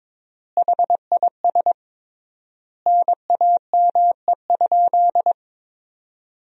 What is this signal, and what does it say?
Morse code, heard as a keyed tone, about 22 wpm, 700 Hz. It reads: HIS NAME?
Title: 3x Word-Spacing - Sentences from Top 200 Words - 22wpm